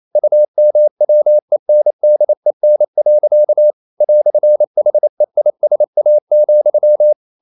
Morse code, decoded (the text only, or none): UMWENDEN. "HEISA,